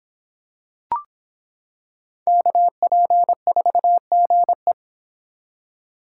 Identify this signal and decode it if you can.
Morse code, heard as a keyed tone, about 25 wpm, 700 Hz. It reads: KP4GE